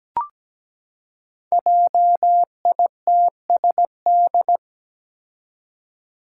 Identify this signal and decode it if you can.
Morse code, heard as a keyed tone, about 17 wpm, 700 Hz. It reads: JITSD
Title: Five Challenging Letter-Number Combinations 17wpm